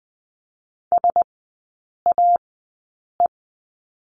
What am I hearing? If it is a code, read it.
Morse code, heard as a keyed tone, about 20 wpm, spaced as for 8 wpm, 700 Hz. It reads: SAE
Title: 035 Lesson 7e - Letter s - Three Characters 20wpm